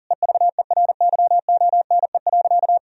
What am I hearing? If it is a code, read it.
Morse code, heard as a keyed tone, about 40 wpm, 700 Hz. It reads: EVERYONE.
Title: Sentences from Top 1000 Words - Review Part 3 - 40wpm